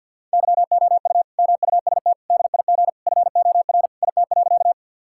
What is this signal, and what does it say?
Morse code, heard as a keyed tone, about 50 wpm, 700 Hz. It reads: YOU MUST DIG FOR IT.
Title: Sentences from Top 1600 Words 50wpm